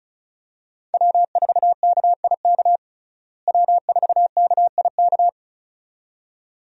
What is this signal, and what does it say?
Morse code, heard as a keyed tone, about 34 wpm, 700 Hz. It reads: W4KIK W4KIK
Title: USA Call Signs - 5 Characters - Repeated 2x - 35wpm